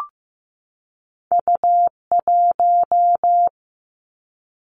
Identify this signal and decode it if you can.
Morse code, heard as a keyed tone, about 15 wpm, 700 Hz. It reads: U1
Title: Two Challenging Letter-Number Combinations 15wpm